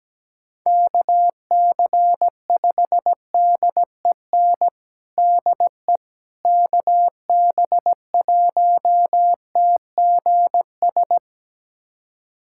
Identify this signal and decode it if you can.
Morse code, heard as a keyed tone, about 17 wpm, 700 Hz. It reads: KC5DEN DE KB1TGS
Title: USA Callsign DE Callsign - 6 Characters - 17wpm